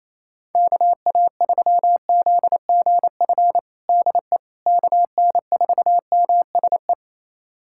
Morse code, heard as a keyed tone, about 28 wpm, 700 Hz. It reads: KA3ZGF DE KN4MSE
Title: USA Callsign DE Callsign 28wpm